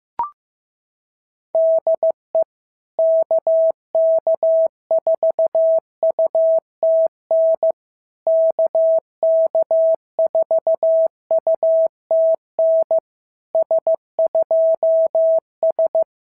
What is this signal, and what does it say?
Morse code, heard as a keyed tone, about 16 wpm, 650 Hz. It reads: DE KK4UTN KK4UTN S2S
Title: SOTA QSO Practice 15wpm